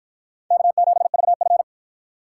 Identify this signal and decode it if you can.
Morse code, heard as a keyed tone, about 48 wpm, 700 Hz. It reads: K7VF